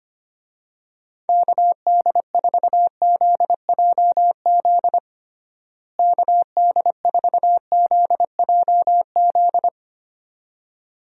Morse code, heard as a keyed tone, about 25 wpm, 700 Hz. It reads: KD4ZJZ KD4ZJZ
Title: USA Call Signs - 6 Characters - Repeated 2x - 25wpm